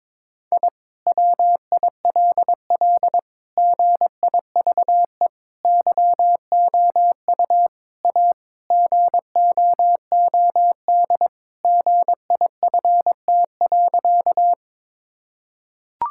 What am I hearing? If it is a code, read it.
Morse code, heard as a keyed tone, about 21 wpm, 700 Hz. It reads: I WILL GIVE YOU A GOOD GIFT.
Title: Sentences from Top 1000 Words - Review Part 6 - 22wpm